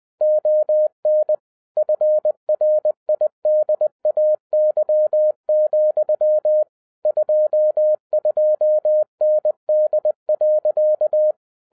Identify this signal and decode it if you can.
Morse code, heard as a keyed tone, about 20 wpm, 610 Hz. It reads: ON FRIDAY, 22ND.
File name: 2011-07-mp3 / n48 JOHN K3TN 20wpm